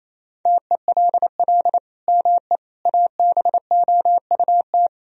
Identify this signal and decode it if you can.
Morse code, heard as a keyed tone, about 27 wpm, 700 Hz. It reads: TELL ME ABOUT
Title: Sentences from Top 1000 Words - Review Part 3 - 28wpm